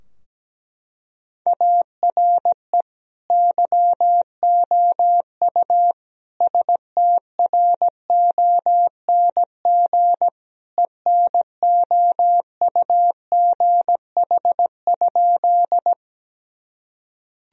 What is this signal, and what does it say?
Morse code, heard as a keyed tone, about 17 wpm, 700 Hz. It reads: ARE YOU STRONG ENOUGH?
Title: Sentences from Top 1000 Words - Review Part 3 - 17wpm